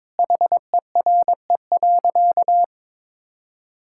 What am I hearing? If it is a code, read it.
Morse code, heard as a keyed tone, about 22 wpm, 700 Hz. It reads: HERE.
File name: practice / 22wpm / Sentences with 3-Word Phrases - 22wpm